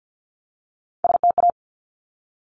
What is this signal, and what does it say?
Morse code, heard as a keyed tone, about 50 wpm, 700 Hz. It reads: STA
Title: Top Three Letter Combinations in Words 50wpm